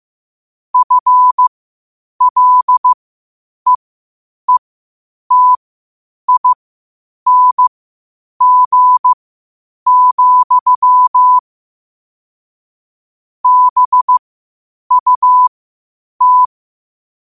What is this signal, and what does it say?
Morse code, heard as a keyed tone, about 15 wpm, spaced as for 8 wpm, 1000 Hz. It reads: FLEETING, BUT
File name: qotdfiles / 8wpm